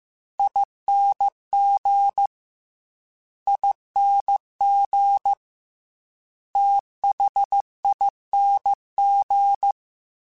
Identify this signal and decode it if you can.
Morse code, heard as a keyed tone, about 15 wpm, 800 Hz. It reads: ING ING THING
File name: Special INGs-mp3s / Special INGs-mp3s / Fast ing @13